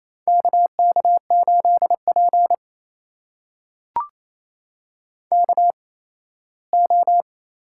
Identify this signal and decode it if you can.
Morse code, heard as a keyed tone, about 29 wpm, spaced as for 17 wpm, 700 Hz. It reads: KK8P KO
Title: USA Call Signs - 4 Characters - 8x Character Spacing - 28wpm